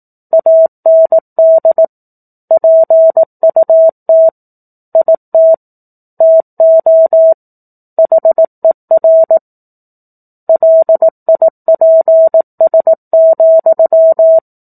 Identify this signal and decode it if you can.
Morse code, heard as a keyed tone, about 18 wpm, 650 Hz. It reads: AND PUT IT TO HER LIPS,
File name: 18 WPM